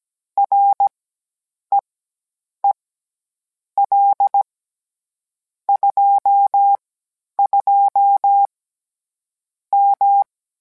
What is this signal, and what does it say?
Morse code, heard as a keyed tone, about 17 wpm, spaced as for 8 wpm, 800 Hz. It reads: REEL 22 M